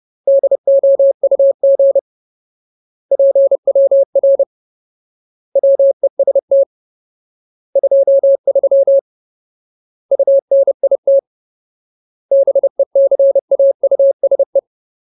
Morse code, heard as a keyed tone, about 30 wpm, 550 Hz. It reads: DOUG PWR WEST 23 UNIT BECAUSE
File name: Ultimate Word List / 30WPM Ultimate